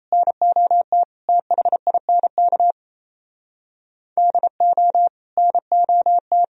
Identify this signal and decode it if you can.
Morse code, heard as a keyed tone, about 34 wpm, 700 Hz. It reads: NOT THINK DO NOT
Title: Speed-Racing - Sets of 3 Words 28wpm